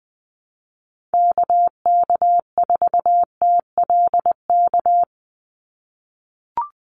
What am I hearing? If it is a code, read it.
Morse code, heard as a keyed tone, about 20 wpm, 700 Hz. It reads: KK4TLK
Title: USA Call Signs 20wpm